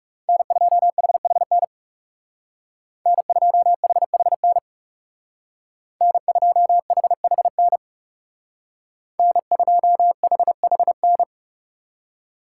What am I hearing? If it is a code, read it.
Morse code, heard as a keyed tone, about 37 wpm, 700 Hz. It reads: N2HHN N2HHN N2HHN N2HHN
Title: Speed Racing - USA 1x3 Call Signs 30wpm